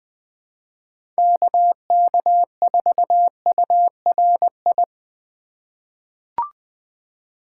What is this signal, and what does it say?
Morse code, heard as a keyed tone, about 20 wpm, 700 Hz. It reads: KK4URI
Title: USA Call Signs - 4x Character Spacing - 20wpm